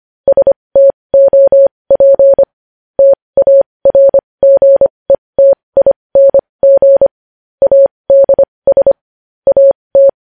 Morse code, heard as a keyed tone, about 25 wpm, spaced as for 20 wpm, 550 Hz. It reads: STOP TARGETING ADS AT
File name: podcast / News headlines-1x-25x20